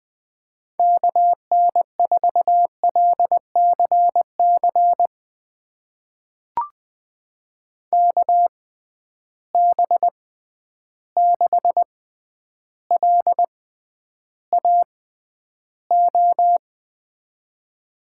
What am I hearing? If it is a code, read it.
Morse code, heard as a keyed tone, about 20 wpm, spaced as for 10 wpm, 700 Hz. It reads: KN4LCC KB6LAO
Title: USA Call Signs - 6 Characters - 6x Character Spacing - 20wpm